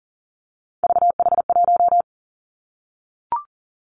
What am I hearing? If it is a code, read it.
Morse code, heard as a keyed tone, about 42 wpm, 700 Hz. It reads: VH1
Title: Three Challenging Letter-Number Combinations 40wpm